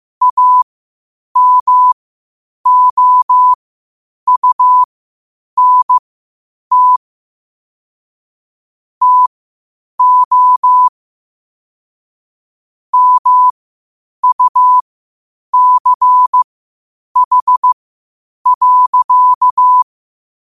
Morse code, heard as a keyed tone, about 15 wpm, spaced as for 8 wpm, 1000 Hz. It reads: AMOUNT TO MUCH.
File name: qotdfiles / 8wpm